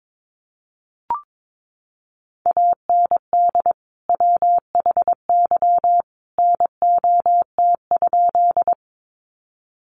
Morse code, heard as a keyed tone, about 22 wpm, 700 Hz. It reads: AND WHY NOT?
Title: Sentences from Top 600 Words 22wpm